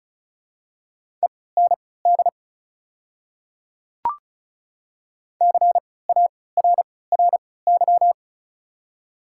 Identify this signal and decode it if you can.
Morse code, heard as a keyed tone, about 34 wpm, spaced as for 20 wpm, 700 Hz. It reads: END CARRY
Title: 3x Character Spacing - Top 300 Words - 35wpm